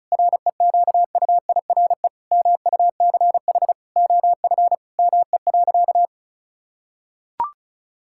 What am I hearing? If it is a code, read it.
Morse code, heard as a keyed tone, about 34 wpm, 700 Hz. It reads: REQUIRE MUCH OF ME.
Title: Sentences from Top 1000 Words - Review Part 6 - 35wpm